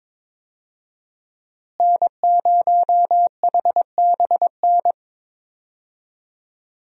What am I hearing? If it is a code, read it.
Morse code, heard as a keyed tone, about 22 wpm, 700 Hz. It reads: N0HBN
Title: USA Call Signs - 5 Characters - Repeated 3x - 22wpm